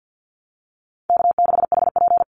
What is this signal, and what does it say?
Morse code, heard as a keyed tone, about 50 wpm, 700 Hz. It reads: K6HP
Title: Easy Call Signs Repeated 3x - 50wpm